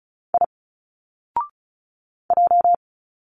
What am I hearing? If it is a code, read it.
Morse code, heard as a keyed tone, about 34 wpm, 700 Hz. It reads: I J
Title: Single Letters 35wpm